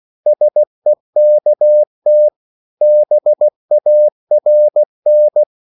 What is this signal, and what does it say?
Morse code, heard as a keyed tone, about 16 wpm, 600 Hz. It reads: SEKT BARN
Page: Korte danske ord 16wpm